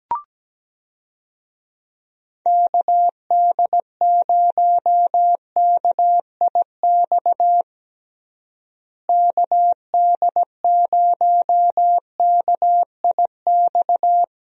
Morse code, heard as a keyed tone, about 17 wpm, 700 Hz. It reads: KD0KIX KD0KIX
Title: USA Call Signs - 6 Characters - Repeated 3x - 17wpm